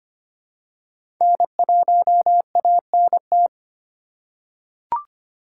Morse code, heard as a keyed tone, about 25 wpm, 700 Hz. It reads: N1ANT
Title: USA Call Signs - 5 Characters - 4x Character Spacing - 25wpm